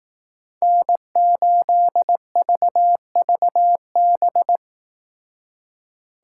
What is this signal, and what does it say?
Morse code, heard as a keyed tone, about 18 wpm, 700 Hz. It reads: N8VVB
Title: Speed Racing - USA Call Signs 15wpm